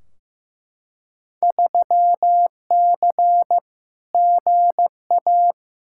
Morse code, heard as a keyed tone, about 15 wpm, 700 Hz. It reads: 3C GA
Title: Field Day Exchanges Encore - 15wpm